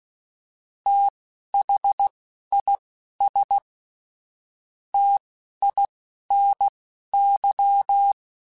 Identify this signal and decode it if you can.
Morse code, heard as a keyed tone, about 16 wpm, spaced as for 12 wpm, 800 Hz. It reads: THIS TINY